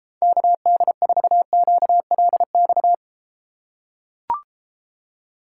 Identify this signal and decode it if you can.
Morse code, heard as a keyed tone, about 30 wpm, 700 Hz. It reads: KD4QLX